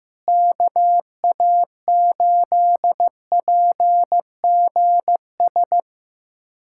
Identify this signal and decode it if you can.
Morse code, heard as a keyed tone, about 15 wpm, 700 Hz. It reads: KA8PGS
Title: USA Callsign DE Callsign - 6 Characters - 15wpm